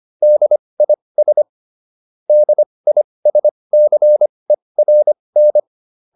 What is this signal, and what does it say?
Morse code, heard as a keyed tone, about 25 wpm, spaced as for 20 wpm, 600 Hz. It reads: DIS DISCERN
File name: DIS7 20